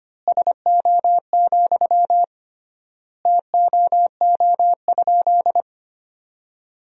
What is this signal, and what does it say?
Morse code, heard as a keyed tone, about 25 wpm, 700 Hz. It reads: SO, TOO?
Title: 3x Word-Spacing - Sentences from Top 200 Words - 25wpm